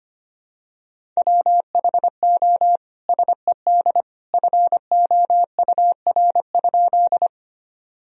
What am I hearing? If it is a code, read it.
Morse code, heard as a keyed tone, about 25 wpm, 700 Hz. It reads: WHO SED FOUR?
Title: Sentences from Top 300 Words and Abbreviations 25wpm